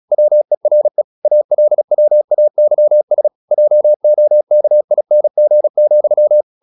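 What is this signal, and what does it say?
Morse code, heard as a keyed tone, about 36 wpm, 600 Hz. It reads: WERE ALWAYS JOKING,
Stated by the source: Morse practice at 36 wpm, text of Ken Follett's novel Never, from CW med Gnister.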